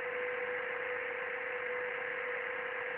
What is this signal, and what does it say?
PSK31, about 31.25 baud.
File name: ARCHIV / BPSK CW / OK1IF-OK1ANG- -1230CET-3766.5-3767